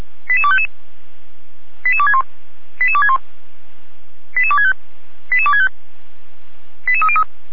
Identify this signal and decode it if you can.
Five-tone selective call (ZVEI), heard as quick runs of five tones, tones of about 70 ms. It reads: 88255 88261 88261 88265 88265 88303